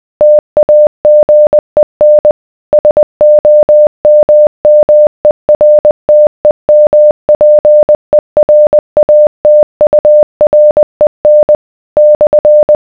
Morse code, heard as a keyed tone, about 20 wpm, 600 Hz. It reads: TAGEN SOMMERTEMPERATUREN /